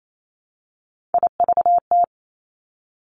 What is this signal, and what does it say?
Morse code, heard as a keyed tone, about 28 wpm, 700 Hz. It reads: IVT
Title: Three Challenging Letter-Number Combinations 28wpm